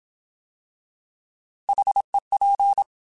Morse code, heard as a keyed tone, about 27 wpm, 800 Hz. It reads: HEP
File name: Alphabetical / Alphabetical mp3s / Hs @25